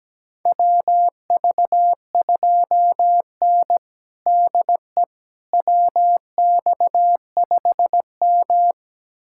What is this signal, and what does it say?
Morse code, heard as a keyed tone, about 17 wpm, 700 Hz. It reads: WV2N DE WX5M
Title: USA Callsign DE Callsign - 4 Characters - 17wpm